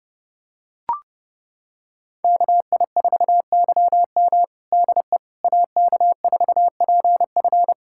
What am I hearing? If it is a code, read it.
Morse code, heard as a keyed tone, about 29 wpm, 700 Hz. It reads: KI4YM DE AK4PF
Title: USA Callsign DE Callsign - 5 Characters - 30wpm